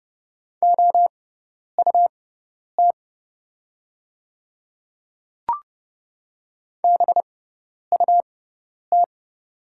Morse code, heard as a keyed tone, about 30 wpm, spaced as for 10 wpm, 700 Hz. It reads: OUT BUT
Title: 6x Character Spacing - Top 100 Words - 30wpm